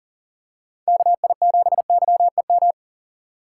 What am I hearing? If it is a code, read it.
Morse code, heard as a keyed tone, about 40 wpm, 700 Hz. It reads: KI7YEM
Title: USA Call Signs - 6 Characters - 3x Character Spacing - 40wpm